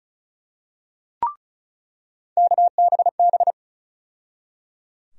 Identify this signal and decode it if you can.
Morse code, heard as a keyed tone, about 34 wpm, 700 Hz. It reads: KBB
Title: USA Call Sign Postfixes - 3 Letters 35wpm